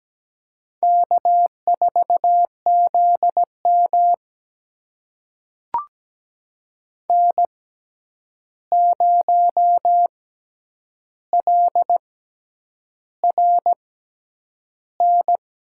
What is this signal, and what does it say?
Morse code, heard as a keyed tone, about 17 wpm, spaced as for 8 wpm, 700 Hz. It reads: K4ZM N0LRN
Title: USA Call Signs - 6x Character Spacing - 17wpm